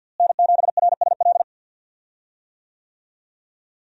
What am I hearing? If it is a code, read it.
Morse code, heard as a keyed tone, about 52 wpm, 700 Hz. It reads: N7RSL
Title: USA Call Signs - 5 Characters - Repeated 3x - 50wpm